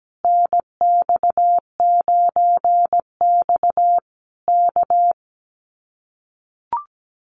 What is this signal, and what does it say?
Morse code, heard as a keyed tone, about 17 wpm, 700 Hz. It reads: NX9X K
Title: Common QSO Phrases Encore 17wpm